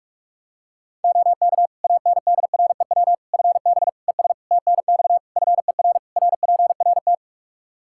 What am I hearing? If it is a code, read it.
Morse code, heard as a keyed tone, about 42 wpm, 700 Hz. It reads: OK ANDREW FB ES TNX FER RPRT
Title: Practice QSO - Exchange 2 - Info - 45wpm